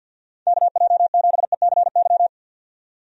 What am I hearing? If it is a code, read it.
Morse code, heard as a keyed tone, about 49 wpm, 700 Hz. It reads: KJ7ECY
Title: USA Call Signs - 6 Characters - 50wpm